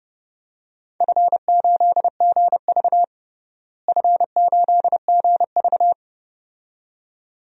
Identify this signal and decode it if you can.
Morse code, heard as a keyed tone, about 32 wpm, 700 Hz. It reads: F8GV F8GV
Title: DXCC-100 Call Signs - 4 Characters - Repeated 2x - 30wpm